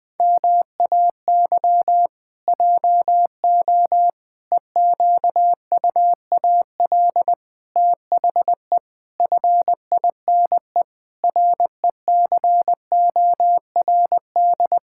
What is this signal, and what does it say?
Morse code, heard as a keyed tone, about 19 wpm, 700 Hz. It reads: MAY JO EQUAL THE FINE RECORD
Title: Pangram Challenge v2 - 20wpm